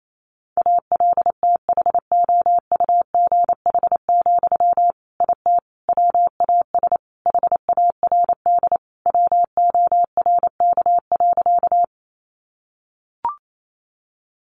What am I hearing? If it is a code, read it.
Morse code, heard as a keyed tone, about 28 wpm, 700 Hz. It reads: ALTHOUGH, IT WAS HARD WORK.
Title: Sentences from Top 600 Words 28wpm